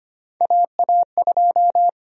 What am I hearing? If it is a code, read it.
Morse code, heard as a keyed tone, about 25 wpm, 700 Hz. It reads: AA2
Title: USA Call Sign Prefixes - 25wpm